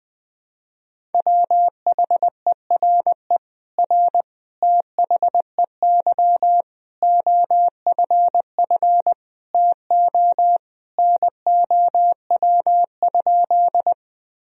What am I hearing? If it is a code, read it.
Morse code, heard as a keyed tone, about 20 wpm, 700 Hz. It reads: WHERE R THEY OFF TO NOW?
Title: Sentences from Top 200 Words and Abbreviations 20wpm